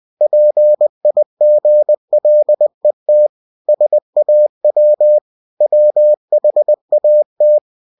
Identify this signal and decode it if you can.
Morse code, heard as a keyed tone, about 20 wpm, 600 Hz. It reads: PIGLET SAW WHAT
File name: cw-books / winnie-the-pooh / winnie-20wpm-page-0043